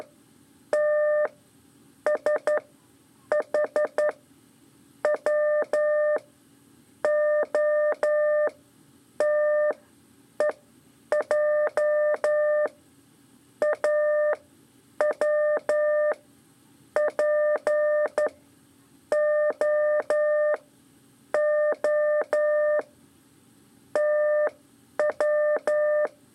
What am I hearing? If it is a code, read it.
Morse code, heard as a keyed tone, about 12 wpm, 600 Hz. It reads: T S H W O T E J A W P O O T W
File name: 3 lezione CW 2 AIR